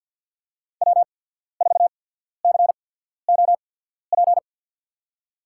Morse code, heard as a keyed tone, about 54 wpm, spaced as for 14 wpm, 700 Hz. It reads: W4COP
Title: USA Call Signs - 5 Characters - 8x Character Spacing - 50wpm